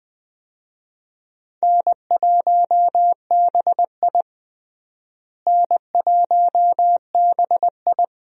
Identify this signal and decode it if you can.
Morse code, heard as a keyed tone, about 20 wpm, 700 Hz. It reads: N1BI N1BI
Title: USA Call Signs - 4 Characters - Repeated 2x - 20wpm